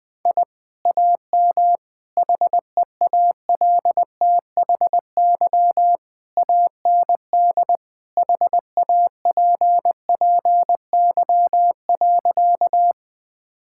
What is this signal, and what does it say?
Morse code, heard as a keyed tone, about 20 wpm, 700 Hz. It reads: I AM HEALTHY AND HAPPY.
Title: Sentences from Top 1000 Words - Review Part 5 - 20wpm